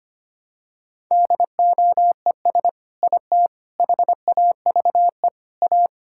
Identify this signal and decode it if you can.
Morse code, heard as a keyed tone, about 25 wpm, 700 Hz. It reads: DOES IT HAVE A
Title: Sentences from Top 1000 Words - Review Part 5 - 25wpm